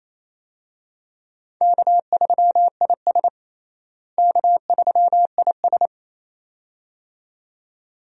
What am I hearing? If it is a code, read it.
Morse code, heard as a keyed tone, about 28 wpm, 700 Hz. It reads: K3IS K3IS
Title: USA Call Signs - 4 Characters - Repeated 2x - 28wpm